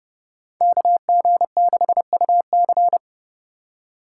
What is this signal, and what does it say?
Morse code, heard as a keyed tone, about 28 wpm, 700 Hz. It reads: KG6UC